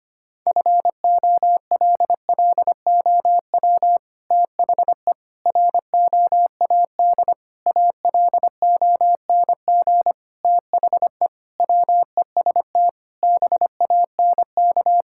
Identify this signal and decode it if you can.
Morse code, heard as a keyed tone, about 25 wpm, 700 Hz. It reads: FOLLOW THE ROAD ALONG THE WEST BANK
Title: Sentences from Top 1000 Words - Review Part 6 - 25wpm